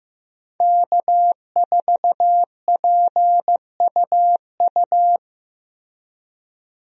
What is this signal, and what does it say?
Morse code, heard as a keyed tone, about 16 wpm, 700 Hz. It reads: K4PUU
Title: USA Call Signs - 5 Characters - Repeated 3x - 15wpm